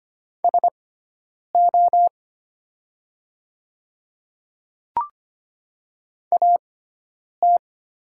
Morse code, heard as a keyed tone, about 25 wpm, spaced as for 9 wpm, 700 Hz. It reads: SO AT
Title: 6x Character Spacing - Top 100 Words - 25wpm